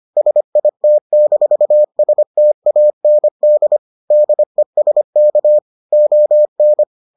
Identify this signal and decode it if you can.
Morse code, heard as a keyed tone, about 25 wpm, 600 Hz. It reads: SIT-STAND DESK ON